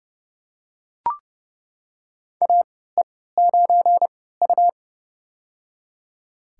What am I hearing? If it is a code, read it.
Morse code, heard as a keyed tone, about 30 wpm, spaced as for 17 wpm, 700 Hz. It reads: AE9U
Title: USA Call Signs - 4 Characters - 3x Character Spacing - 30wpm